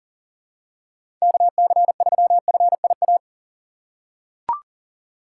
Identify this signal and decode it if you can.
Morse code, heard as a keyed tone, about 40 wpm, 700 Hz. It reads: KC3FIA
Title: USA Call Signs - 6 Characters - 6x Character Spacing - 40wpm